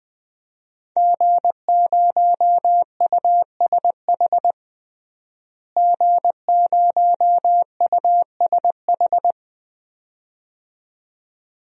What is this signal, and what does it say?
Morse code, heard as a keyed tone, about 21 wpm, 700 Hz. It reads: G0USH G0USH
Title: DXCC-100 Call Signs - 5 Characters - Repeated 2x - 20wpm